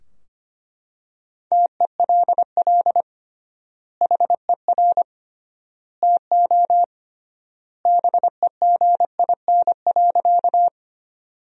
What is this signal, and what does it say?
Morse code, heard as a keyed tone, about 25 wpm, 700 Hz. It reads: TELL HER TO BEGIN.
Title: 3x Word-Spacing - Sentences from Top 300 Words - 25wpm